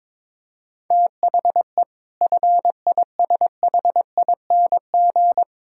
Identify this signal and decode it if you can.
Morse code, heard as a keyed tone, about 21 wpm, 700 Hz. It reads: THE FISHING
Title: Sentences from Top 1000 Words - Review Part 5 - 22wpm